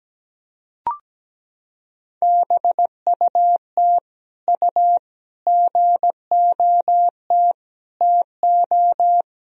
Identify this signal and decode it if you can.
Morse code, heard as a keyed tone, about 17 wpm, 700 Hz. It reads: BUT U GOT TO
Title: Sentences from Top 300 Words and Abbreviations 17wpm